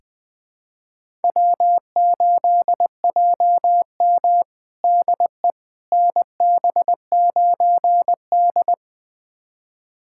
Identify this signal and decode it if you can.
Morse code, heard as a keyed tone, about 20 wpm, 700 Hz. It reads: W8JM DE NB9D
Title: USA Callsign DE Callsign - 4 Characters - 20wpm